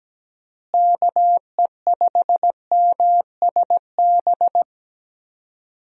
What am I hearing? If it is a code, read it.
Morse code, heard as a keyed tone, about 17 wpm, 700 Hz. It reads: KE5MSB
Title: USA Call Signs - 6 Characters - Repeated 3x - 17wpm